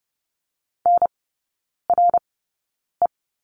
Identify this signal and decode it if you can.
Morse code, heard as a keyed tone, about 30 wpm, spaced as for 9 wpm, 700 Hz. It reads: NRE
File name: course / 30:10wpm / 087 Lesson 15e - Number 5 - Three Characters 30wpm